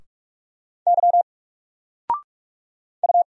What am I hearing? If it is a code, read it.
Morse code, heard as a keyed tone, about 45 wpm, 700 Hz. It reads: Y U
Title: Single Letter-Number 45wpm